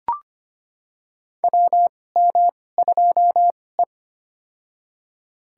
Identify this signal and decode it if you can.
Morse code, heard as a keyed tone, about 25 wpm, spaced as for 18 wpm, 700 Hz. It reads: WM2E
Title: USA Call Signs - 4 Characters - 2x Character Spacing - 25wpm